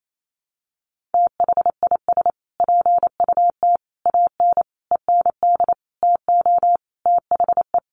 Morse code, heard as a keyed tone, about 28 wpm, 700 Hz. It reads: THIS PUT AN END TO THE
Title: Sentences from Top 1100 Words 28wpm